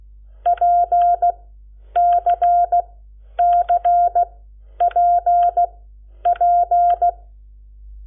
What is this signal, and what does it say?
Morse code, heard as a keyed tone, about 16 wpm, spaced as for 10 wpm, 700 Hz. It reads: PCCPP
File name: Lecciones / CLASE Nº7